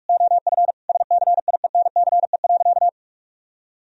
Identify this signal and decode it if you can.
Morse code, heard as a keyed tone, about 42 wpm, 700 Hz. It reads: OF SCIENCE.
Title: Sentences from Top 1000 Words - Review Part 3 - 45wpm